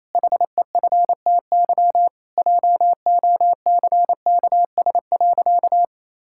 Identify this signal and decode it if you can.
Morse code, heard as a keyed tone, about 28 wpm, 700 Hz. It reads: HEFTY JOCKS.
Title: Pangram Challenge v2 - 28wpm